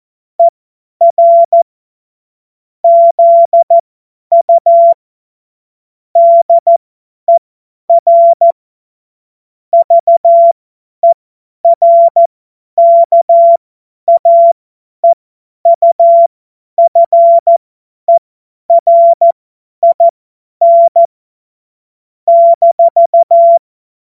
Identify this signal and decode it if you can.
Morse code, heard as a keyed tone, about 14 wpm, spaced as for 10 wpm, 680 Hz. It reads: ER ZU DER VERKAEUFERIN -
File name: Morsekurs / Witze / Witz 13. Januar 2026 50 F70